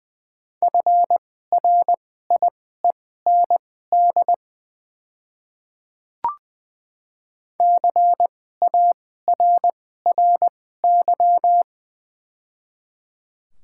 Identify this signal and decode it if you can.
Morse code, heard as a keyed tone, about 20 wpm, spaced as for 14 wpm, 690 Hz. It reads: FRIEND CARRY
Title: 2x Character Spacing - Top 500 Words - 20wpm